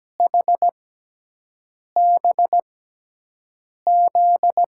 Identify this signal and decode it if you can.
Morse code, heard as a keyed tone, about 17 wpm, spaced as for 6 wpm, 700 Hz. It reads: HBZ
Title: USA Call Signs - 6 Characters - 6x Character Spacing - 17wpm